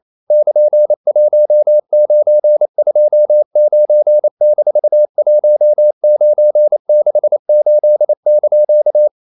Morse code, heard as a keyed tone, about 28 wpm, 600 Hz. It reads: <KN>1929-1968)
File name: CW / Practice / EnglishQuotations0128